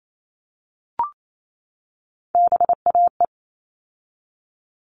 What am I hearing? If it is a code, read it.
Morse code, heard as a keyed tone, about 28 wpm, 700 Hz. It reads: BAE
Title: All Cut Numbers - 3-Digit 28wpm